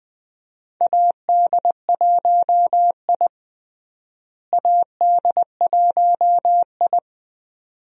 Morse code, heard as a keyed tone, about 20 wpm, 700 Hz. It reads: AD1I AD1I
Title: USA Call Signs - 4 Characters - Repeated 2x - 20wpm